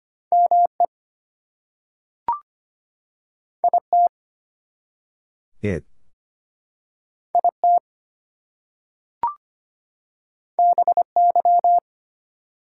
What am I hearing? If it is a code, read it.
Morse code, heard as a keyed tone, about 25 wpm, 700 Hz. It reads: ME IT IT BY
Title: Two Letter Words - 25wpm